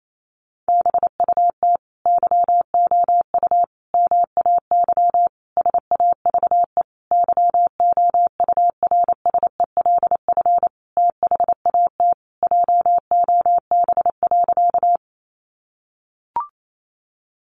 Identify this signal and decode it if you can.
Morse code, heard as a keyed tone, about 28 wpm, 700 Hz. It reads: BUT YOU MAY SAVE YOURSELF THAT JOB.
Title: Sentences from Top 1100 Words 28wpm